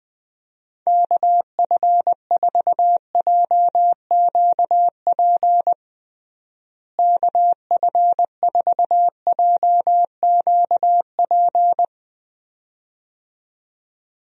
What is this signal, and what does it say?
Morse code, heard as a keyed tone, about 20 wpm, 700 Hz. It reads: KF4JQP KF4JQP
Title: USA Call Signs - 6 Characters - Repeated 2x - 20wpm